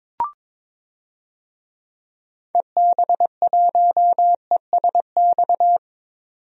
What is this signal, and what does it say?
Morse code, heard as a keyed tone, about 22 wpm, 700 Hz. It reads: EB1ESX
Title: DXCC-100 Call Signs - 6 Characters - 22wpm